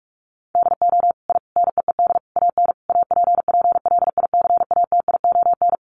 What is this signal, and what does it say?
Morse code, heard as a keyed tone, about 44 wpm, 700 Hz. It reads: DO I NEED AN APPLICATION
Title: Sentences from Top 1200 Words 45wpm